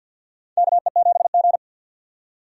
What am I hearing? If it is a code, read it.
Morse code, heard as a keyed tone, about 52 wpm, 700 Hz. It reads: KE7G
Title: USA Call Signs - 4 Characters - 50wpm